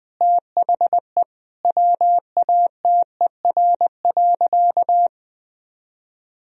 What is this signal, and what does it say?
Morse code, heard as a keyed tone, about 20 wpm, 700 Hz. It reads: THE WATER.
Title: Sentences from Top 500 Words and Abbreviations 20wpm